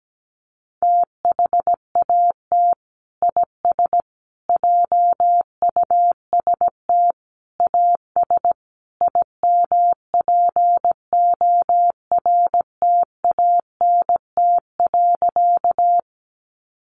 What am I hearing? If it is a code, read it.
Morse code, heard as a keyed tone, about 17 wpm, 700 Hz. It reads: THAT IS JUST AS IMPORTANT.
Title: Sentences from Top 600 Words 17wpm